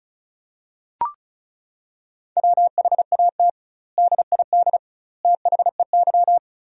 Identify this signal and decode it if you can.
Morse code, heard as a keyed tone, about 36 wpm, 700 Hz. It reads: WHAT DID THEY
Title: 2x Word-Spacing - Sentences from Top 100 Words - 35wpm